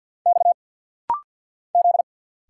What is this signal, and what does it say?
Morse code, heard as a keyed tone, about 50 wpm, 700 Hz. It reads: X Z
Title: Single Letters - Rapid-Fire 50wpm